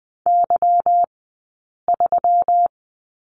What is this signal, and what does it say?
Morse code, heard as a keyed tone, about 20 wpm, spaced as for 8 wpm, 700 Hz. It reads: Y3
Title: 197 Lesson 32d - Letter k - Two Characters 20wpm